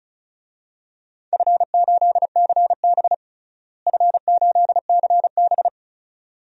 Morse code, heard as a keyed tone, about 34 wpm, 700 Hz. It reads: F8CB F8CB
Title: DXCC-100 Call Signs - 4 Characters - Repeated 2x - 35wpm